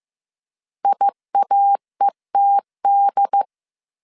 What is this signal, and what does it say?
Morse code, heard as a keyed tone, about 15 wpm, 800 Hz. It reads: IAETD